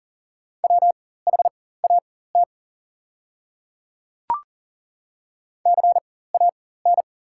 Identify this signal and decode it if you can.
Morse code, heard as a keyed tone, about 40 wpm, spaced as for 19 wpm, 690 Hz. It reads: WHAT CAN
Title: 4x Character Spacing - Top 100 Words - 40wpm